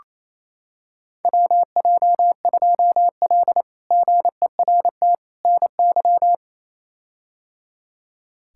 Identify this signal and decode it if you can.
Morse code, heard as a keyed tone, about 27 wpm, 700 Hz. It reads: WJ2L GERT NY
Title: K1USN SST QSO Practice - 28wpm